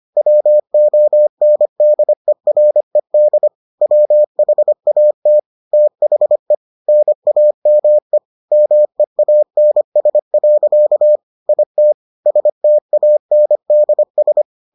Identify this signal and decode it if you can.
Morse code, heard as a keyed tone, about 25 wpm, 600 Hz. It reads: WONDERED WHAT THE NAME MEANS. IT STANDS